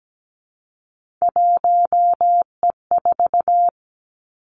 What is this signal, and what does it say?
Morse code, heard as a keyed tone, about 17 wpm, 700 Hz. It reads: 1E4
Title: Three Challenging Letter-Number Combinations 17wpm